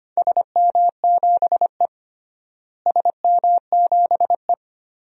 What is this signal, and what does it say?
Morse code, heard as a keyed tone, about 25 wpm, 700 Hz. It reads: SM7E SM7E
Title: DXCC-100 Call Signs - 4 Characters - Repeated 3x - 25wpm